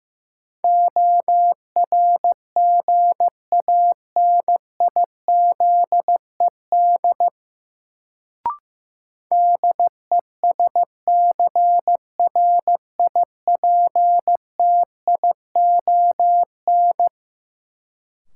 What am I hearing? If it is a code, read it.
Morse code, heard as a keyed tone, about 15 wpm, 700 Hz. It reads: ORGANIZED DESCRIPTION
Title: Sentences from Top 1300 Words - New Word Drill - 15wpm